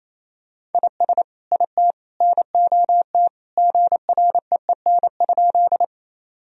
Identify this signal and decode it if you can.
Morse code, heard as a keyed tone, about 28 wpm, 700 Hz. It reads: IS IT NOT GREEN?
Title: Sentences from Top 1000 Words - Review Part 2 - 28wpm